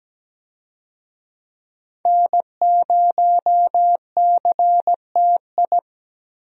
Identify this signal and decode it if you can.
Morse code, heard as a keyed tone, about 17 wpm, 700 Hz. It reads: N0CTI
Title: USA Call Signs - 5 Characters - Repeated 3x - 17wpm